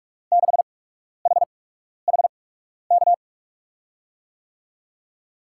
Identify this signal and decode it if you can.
Morse code, heard as a keyed tone, about 47 wpm, spaced as for 12 wpm, 700 Hz. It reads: 6HHK
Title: USA Call Signs - 6 Characters - 8x Character Spacing - 45wpm